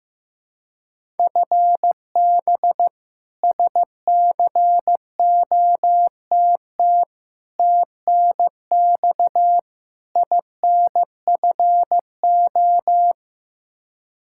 Morse code, heard as a keyed tone, about 15 wpm, 700 Hz. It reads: FB SCOTT TNX INFO
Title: K1USN SST QSO Practice 15wpm